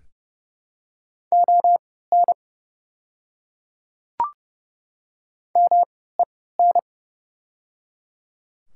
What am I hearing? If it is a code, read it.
Morse code, heard as a keyed tone, about 30 wpm, spaced as for 17 wpm, 700 Hz. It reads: ON MEN
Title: 3x Character Spacing - Top 200 Words - 30wpm